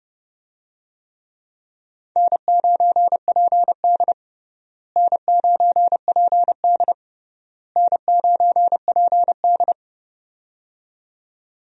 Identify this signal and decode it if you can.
Morse code, heard as a keyed tone, about 30 wpm, 700 Hz. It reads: N9PD N9PD N9PD
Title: USA Call Signs - 4 Characters - Repeated 3x - 30wpm